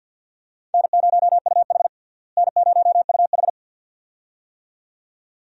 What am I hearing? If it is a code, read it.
Morse code, heard as a keyed tone, about 47 wpm, 700 Hz. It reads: N0UH N0UH
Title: USA Call Signs - 4 Characters - Repeated 2x - 50wpm